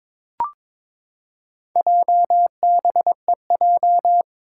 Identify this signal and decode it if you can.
Morse code, heard as a keyed tone, about 22 wpm, 700 Hz. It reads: JBEJ
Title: Four Challenging Letter-Number Combinations 22wpm